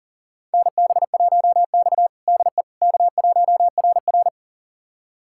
Morse code, heard as a keyed tone, about 40 wpm, 700 Hz. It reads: NB1X DE K1RR
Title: USA Callsign DE Callsign - 4 Characters - 40wpm